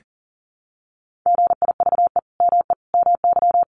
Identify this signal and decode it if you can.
Morse code, heard as a keyed tone, about 40 wpm, 700 Hz. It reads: GIVE ME MY
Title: Sentences from Top 200 Words 40wpm